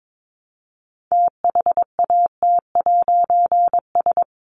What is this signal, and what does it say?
Morse code, heard as a keyed tone, about 22 wpm, 700 Hz. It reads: THAT'S
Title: A Sound Of Thunder 22wpm